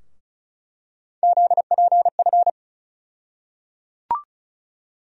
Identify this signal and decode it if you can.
Morse code, heard as a keyed tone, about 36 wpm, 700 Hz. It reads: ZPF
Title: USA Call Sign Postfixes - 3 Letters 35wpm